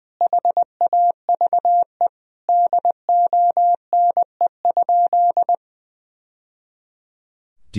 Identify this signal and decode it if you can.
Morse code, heard as a keyed tone, about 20 wpm, 700 Hz. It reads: HAVE DONE?
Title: Sentences from Top 1000 Words - Review Part 3 - 20wpm